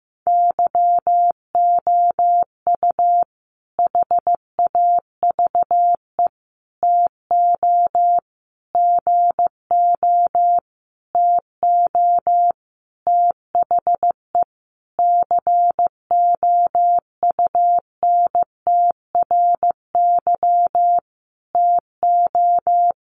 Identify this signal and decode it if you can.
Morse code, heard as a keyed tone, about 15 wpm, 700 Hz. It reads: YOU HAVE TO GO TO THE COUNTRY TO
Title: Sentences from Top 1300 Words 15wpm